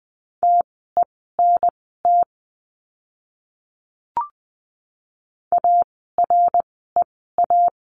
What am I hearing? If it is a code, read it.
Morse code, heard as a keyed tone, about 20 wpm, spaced as for 14 wpm, 700 Hz. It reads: TENT AREA
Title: 054 Lesson 10f - Letter r - Words 20wpm